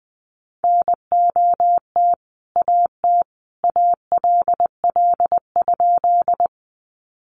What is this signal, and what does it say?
Morse code, heard as a keyed tone, about 20 wpm, 700 Hz. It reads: NOT AT ALL?
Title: Sentences from Top 600 Words 20wpm